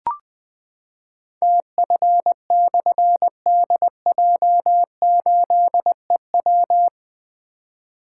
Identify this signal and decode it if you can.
Morse code, heard as a keyed tone, about 20 wpm, 700 Hz. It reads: TF/DJ8EW
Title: Difficult Call Signs 20wpm